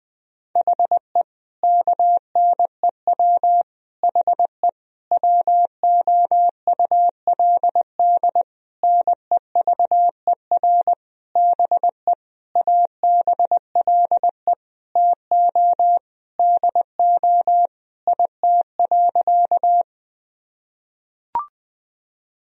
Morse code, heard as a keyed tone, about 21 wpm, 700 Hz. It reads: HE KNEW HE WOULD NEVER BE ABLE TO DO IT.
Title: Sentences from Top 1000 Words - Review Part 2 - 20wpm